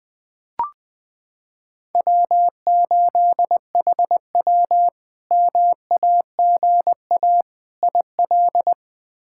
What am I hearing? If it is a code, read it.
Morse code, heard as a keyed tone, about 20 wpm, 700 Hz. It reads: W8HW MAGA IL
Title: K1USN SST QSO Practice - 20wpm